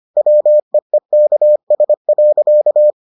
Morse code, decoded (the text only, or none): WEEKS.